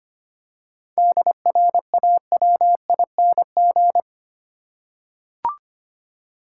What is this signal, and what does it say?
Morse code, heard as a keyed tone, about 25 wpm, 700 Hz. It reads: DRAWING
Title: Seven Letter Words - 25wpm